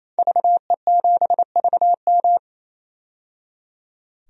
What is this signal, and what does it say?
Morse code, heard as a keyed tone, about 28 wpm, 700 Hz. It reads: VE7VM
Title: DXCC-100 Call Signs - 5 Characters - 28wpm